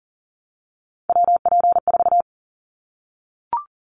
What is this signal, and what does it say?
Morse code, heard as a keyed tone, about 41 wpm, 700 Hz. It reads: WP4
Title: Callsign Prefixes for Top 25 DXCC - 40wpm